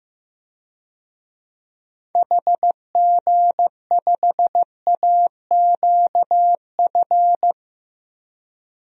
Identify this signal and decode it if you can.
Morse code, heard as a keyed tone, about 15 wpm, 700 Hz. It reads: HG5AQF
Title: DXCC-100 Call Signs - 6 Characters - Repeated 3x - 15wpm